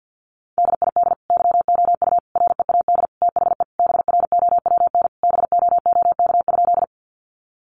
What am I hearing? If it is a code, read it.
Morse code, heard as a keyed tone, about 50 wpm, 700 Hz. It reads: DID YOU READ THE BROWN BOOK?
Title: Sentences from Top 1200 Words 50wpm